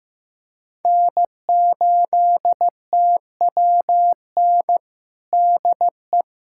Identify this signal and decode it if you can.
Morse code, heard as a keyed tone, about 15 wpm, 700 Hz. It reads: N8TWN DE
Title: USA Callsign DE Callsign 15wpm